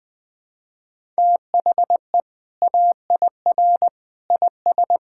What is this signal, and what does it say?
Morse code, heard as a keyed tone, about 20 wpm, 700 Hz. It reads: THE AIR IS
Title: Sentences with 2-Word Phrases - 20wpm